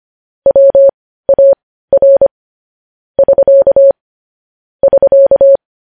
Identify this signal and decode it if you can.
Morse code, heard as a keyed tone, about 25 wpm, spaced as for 15 wpm, 550 Hz. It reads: WAR <SK> <SK>
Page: news-headlines-in-morse-code-at-15-wpm